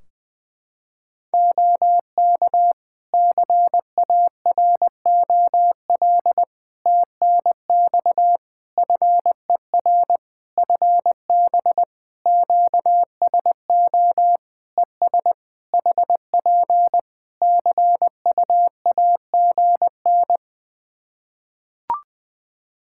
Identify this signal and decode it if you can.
Morse code, heard as a keyed tone, about 20 wpm, 700 Hz. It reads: OK CAROL TNX FER FB QSO ES HP CUAGN
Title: K1USN SST QSO Practice 20wpm